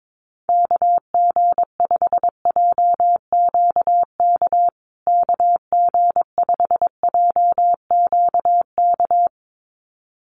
Morse code, heard as a keyed tone, about 22 wpm, 700 Hz. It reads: KG5JQK KG5JQK
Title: USA Call Signs Repeated 2x 22wpm